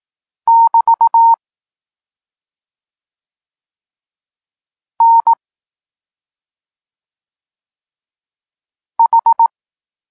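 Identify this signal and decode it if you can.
Morse code, heard as a keyed tone, about 18 wpm, 920 Hz. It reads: <BT> N H
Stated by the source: Morse code lesson for learning the letters K and X, from the SP9MOA Niepołomicki Klub Krótkofalowców course.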